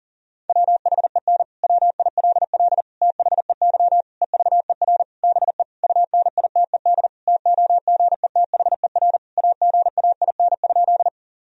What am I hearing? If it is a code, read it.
Morse code, heard as a keyed tone, about 37 wpm, 700 Hz. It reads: WHEN WILL THEY EVER BE UNITED TOGETHER AGAIN?
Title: Sentences from Top 1000 Words - Review Part 4 - 40wpm